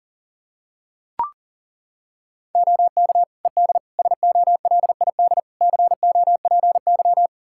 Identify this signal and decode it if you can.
Morse code, heard as a keyed tone, about 40 wpm, 700 Hz. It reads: OK ED SOLID COPY
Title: K1USN SST QSO Practice 40wpm